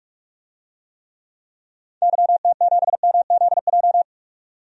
Morse code, heard as a keyed tone, about 41 wpm, 700 Hz. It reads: YT7MZJ
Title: DXCC-100 Call Signs - 6 Characters - 45wpm